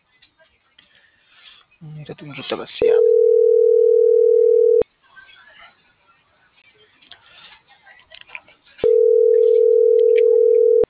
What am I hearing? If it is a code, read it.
Telephone call progress tone (North America), heard as a telephone network's tone, about 440, 480 Hz, with speech. Ringback tone.